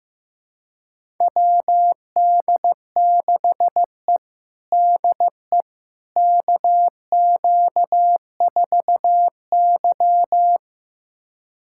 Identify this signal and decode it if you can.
Morse code, heard as a keyed tone, about 16 wpm, 700 Hz. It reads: WD6E DE KQ4Y
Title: USA Callsign DE Callsign - 4 Characters - 15wpm